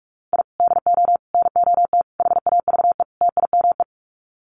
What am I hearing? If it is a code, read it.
Morse code, heard as a keyed tone, about 44 wpm, 700 Hz. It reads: I DO NOT HAVE TIME
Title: Sets of 5 Words Encore 45wpm